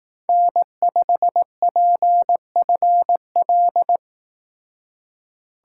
Morse code, heard as a keyed tone, about 19 wpm, 700 Hz. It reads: N5PFL
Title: Speed Racing - USA 1x3 Call Signs 15wpm